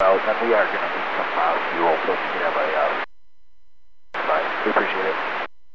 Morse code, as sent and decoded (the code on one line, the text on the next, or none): none